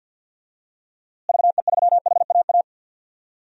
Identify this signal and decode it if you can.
Morse code, heard as a keyed tone, about 52 wpm, 700 Hz. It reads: VE3HAA
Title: DXCC-100 Call Signs - 6 Characters - Repeated 2x - 50wpm